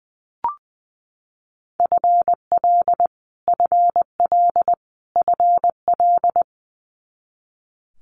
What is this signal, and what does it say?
Morse code, heard as a keyed tone, about 20 wpm, 700 Hz. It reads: FL FL FL
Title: Multi-letter Phonemes Repeated 3x 20wpm